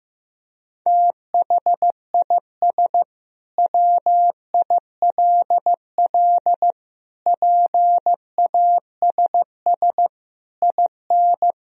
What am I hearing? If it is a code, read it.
Morse code, heard as a keyed tone, about 15 wpm, 700 Hz. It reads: THIS WILL PASS IN
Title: Sentences from Top 400 Words and Abbreviations 15wpm